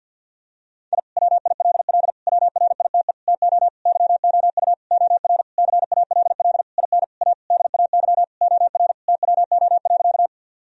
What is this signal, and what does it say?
Morse code, heard as a keyed tone, about 50 wpm, 700 Hz. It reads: I WILL WRITE TO YOU OR CALL IN A DAY OR TWO.
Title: Sentences from Top 1000 Words - Review Part 1 - 50wpm